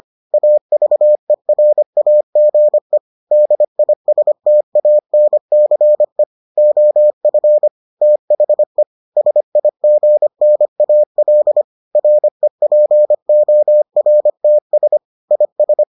Morse code, heard as a keyed tone, about 25 wpm, 600 Hz. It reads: AVERAGE DISTANCE OF THE SIGNAL REPORTS IS